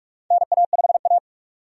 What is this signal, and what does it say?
Morse code, heard as a keyed tone, about 46 wpm, 700 Hz. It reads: NA5A
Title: USA Call Signs - 4 Characters - 45wpm